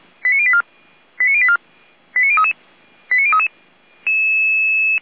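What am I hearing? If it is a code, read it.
Five-tone selective call (ZVEI), heard as quick runs of five tones, tones of about 70 ms. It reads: 89084 89084 89033 89033
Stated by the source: ZVEI five-tone call 89042 received by SDR.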